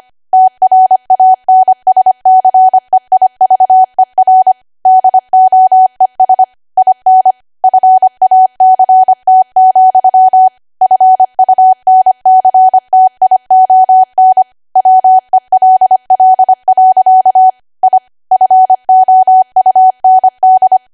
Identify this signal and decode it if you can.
Morse code, heard as a keyed tone, about 25 wpm, 750 Hz. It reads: TRANSCEIVER DOES IN FACT, FUNCTION WELL. I FOUND